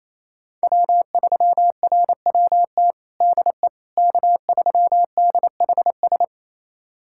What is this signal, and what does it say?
Morse code, heard as a keyed tone, about 28 wpm, 700 Hz. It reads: W3RWT DE K3DHS
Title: USA Callsign DE Callsign - 5 Characters - 28wpm